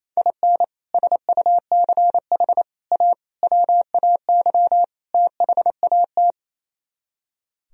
Morse code, heard as a keyed tone, about 34 wpm, 700 Hz. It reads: IN SUCH A WAY THAT
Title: Speed-Racing - Sets of 5 Words 28wpm